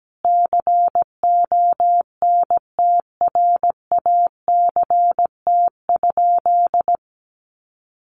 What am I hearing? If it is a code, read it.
Morse code, heard as a keyed tone, about 17 wpm, 700 Hz. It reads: CONTRACT?
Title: Sentences from Top 1400 Words 17wpm